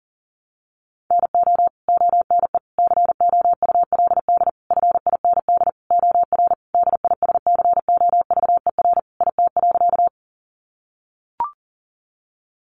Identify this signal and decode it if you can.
Morse code, heard as a keyed tone, about 40 wpm, 700 Hz. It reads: NO ONE COULD FIND OR DISCOVER IT.
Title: Sentences from Top 1100 Words 40wpm